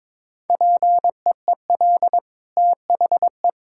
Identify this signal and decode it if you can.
Morse code, heard as a keyed tone, about 21 wpm, 690 Hz. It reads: PEEL THE
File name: practice / 22wpm / Easy Sentences with Hints - 22wpm